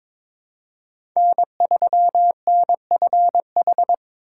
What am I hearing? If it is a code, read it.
Morse code, heard as a keyed tone, about 22 wpm, 700 Hz. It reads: N3NFH
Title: USA Call Signs - 5 Characters - 4x Character Spacing - 22wpm